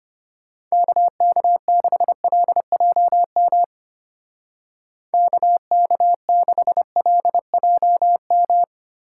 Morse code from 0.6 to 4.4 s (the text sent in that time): KK6LJM